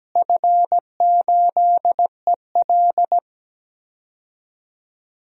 Morse code, heard as a keyed tone, about 17 wpm, 700 Hz. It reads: F8EL
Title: DXCC-100 Call Signs - 4 Characters - Repeated 3x - 17wpm